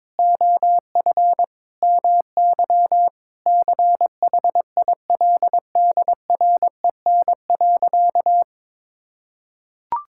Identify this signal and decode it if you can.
Morse code, heard as a keyed tone, about 21 wpm, 700 Hz. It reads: OF MY CHILDREN.
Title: Sentences from Top 1000 Words - Review Part 5 - 22wpm